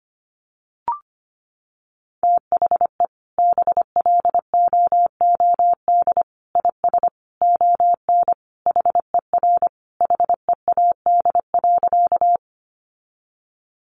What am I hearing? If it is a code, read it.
Morse code, heard as a keyed tone, about 25 wpm, 700 Hz. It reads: THE BLOOD IS ON HER HEAD.
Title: Sentences from Top 1100 Words 25wpm